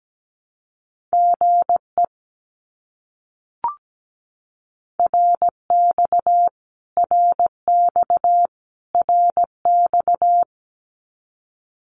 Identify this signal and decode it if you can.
Morse code, heard as a keyed tone, about 17 wpm, 700 Hz. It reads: GE RX RX RX
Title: Abbreviations Repeated 3x - 17wpm